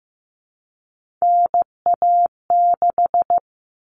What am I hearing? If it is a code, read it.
Morse code, heard as a keyed tone, about 15 wpm, 700 Hz. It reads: NA6
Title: Common Cut Numbers - 3-Digit 15wpm